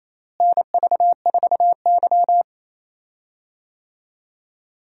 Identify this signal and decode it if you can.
Morse code, heard as a keyed tone, about 28 wpm, 700 Hz. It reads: NV4Y
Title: USA Call Signs - 4 Characters - Repeated 3x - 28wpm